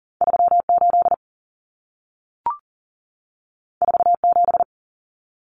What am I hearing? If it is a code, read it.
Morse code, heard as a keyed tone, about 40 wpm, 700 Hz. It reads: 38 47
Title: Numbers - 2-Digit 40wpm